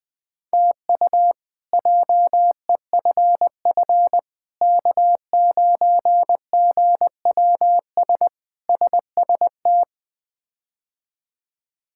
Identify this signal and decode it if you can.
Morse code, heard as a keyed tone, about 20 wpm, 700 Hz. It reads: TU JEFF K9GWS SST
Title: K1USN SST QSO Practice - 20wpm